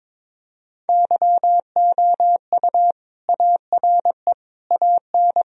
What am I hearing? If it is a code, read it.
Morse code, heard as a keyed tone, about 21 wpm, 700 Hz. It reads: YOU ARE AN
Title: Sentences from Top 1000 Words - Review Part 1 - 22wpm